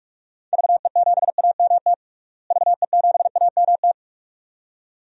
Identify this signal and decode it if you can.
Morse code, heard as a keyed tone, about 41 wpm, 700 Hz. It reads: VE7AMT VE7AMT
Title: DXCC-100 Call Signs - 6 Characters - Repeated 2x - 45wpm